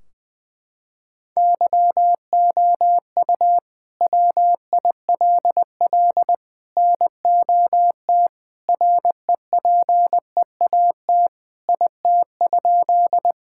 Morse code, heard as a keyed tone, about 20 wpm, 700 Hz. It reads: YOU WILL NOT REPEAT IT?
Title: Sentences from Top 1600 Words 20wpm